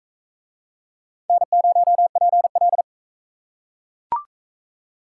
Morse code, heard as a keyed tone, about 40 wpm, 700 Hz. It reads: N0PL